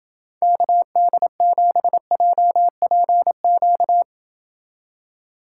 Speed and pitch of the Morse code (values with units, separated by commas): 29 wpm, 700 Hz